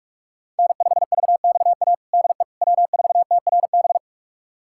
Morse code, heard as a keyed tone, about 46 wpm, 700 Hz. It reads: N5VXA DE W4TRB
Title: USA Callsign DE Callsign - 5 Characters - 45wpm